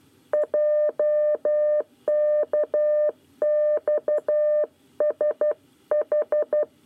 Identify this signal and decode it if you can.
Morse code, heard as a keyed tone, about 12 wpm, 600 Hz. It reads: JKXSH
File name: 9 lezione cw 1